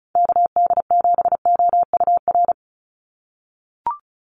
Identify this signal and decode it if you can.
Morse code, heard as a keyed tone, about 34 wpm, 700 Hz. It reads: KD7OUR
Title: USA Call Signs 35wpm